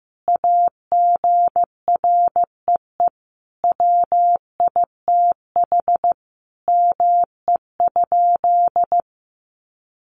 Morse code, heard as a keyed tone, about 15 wpm, 700 Hz. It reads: AGREE WITH ME?
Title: Sentences from Top 1100 Words 15wpm